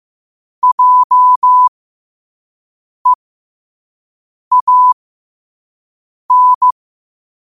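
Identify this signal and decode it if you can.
Morse code, heard as a keyed tone, about 15 wpm, spaced as for 5 wpm, 1000 Hz. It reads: JEAN
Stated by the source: Morse code sent at 5 wpm.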